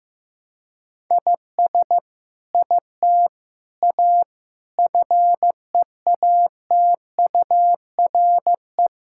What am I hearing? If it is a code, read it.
Morse code, heard as a keyed tone, about 15 wpm, 700 Hz. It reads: IS IT A FEATURE
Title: Sentences from Top 1000 Words - Review Part 5 - 15wpm